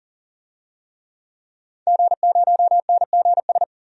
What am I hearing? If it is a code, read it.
Morse code, heard as a keyed tone, about 40 wpm, 700 Hz. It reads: G0NGS
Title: DXCC-100 Call Signs - 5 Characters - 40wpm